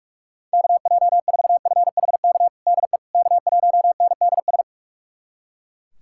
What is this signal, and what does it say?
Morse code, heard as a keyed tone, about 41 wpm, 700 Hz. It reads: KJ4FHK DE K1NDS
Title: USA Callsign DE Callsign 45wpm